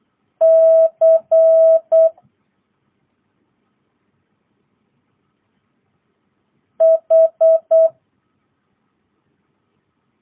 Morse code, heard as a keyed tone, about 8 wpm, 650 Hz. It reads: C H
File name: jh8cbh / 2022cw